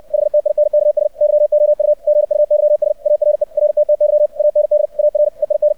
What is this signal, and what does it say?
Morse code, heard as a keyed tone, about 64 wpm, 600 Hz. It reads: HTINKS YOU GUYS ARE RTTY AND ARE IN
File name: archived / CRSnet